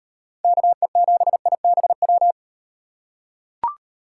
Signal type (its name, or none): Morse code